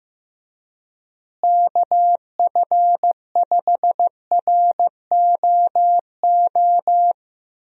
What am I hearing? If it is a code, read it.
Morse code, heard as a keyed tone, about 15 wpm, 700 Hz. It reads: KF5ROO
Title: USA Call Signs - 6 Characters - Repeated 3x - 15wpm